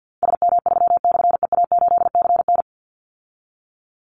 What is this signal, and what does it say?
Morse code, heard as a keyed tone, about 50 wpm, 700 Hz. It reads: SM3/EA8CN
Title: Call Signs Repeated 2x - 50wpm